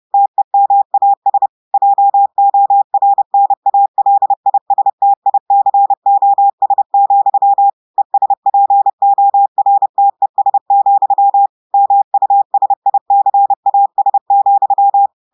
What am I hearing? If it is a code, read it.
Morse code, heard as a keyed tone, about 30 wpm, 800 Hz. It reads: TEMAS JORNALISTICOS, ESPORTES, MUSICAS,
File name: audios / portugais / portugais texte en clair 2 vitesse 30 mots minutes